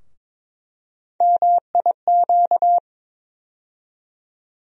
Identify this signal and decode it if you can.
Morse code, heard as a keyed tone, about 22 wpm, 700 Hz. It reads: MIQ
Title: USA Call Sign Postfixes - 3 Letters 22wpm